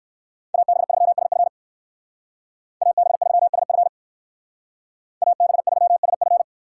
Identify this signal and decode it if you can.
Morse code, heard as a keyed tone, about 54 wpm, 690 Hz. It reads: AB3SF AB3SF AB3SF
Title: Speed Racing - USA 2x2 Call Signs 45wpm